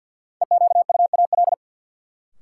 Morse code, heard as a keyed tone, about 52 wpm, 700 Hz. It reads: EQUAL
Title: Sentences from Top 1400 Words - New Word Drill - 50wpm